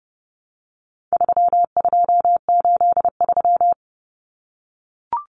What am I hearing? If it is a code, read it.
Morse code, heard as a keyed tone, about 30 wpm, 700 Hz. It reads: 3283
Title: Common Cut Numbers - 4-Digit 30wpm